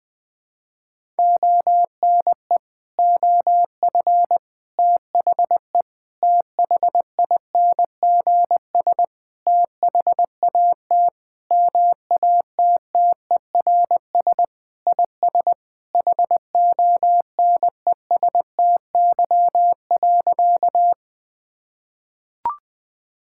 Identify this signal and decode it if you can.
Morse code, heard as a keyed tone, about 20 wpm, 700 Hz. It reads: ONE OF THE THINGS THAT MATTERS IS HONESTY.
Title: Sentences with 5-Word Phrases - 20wpm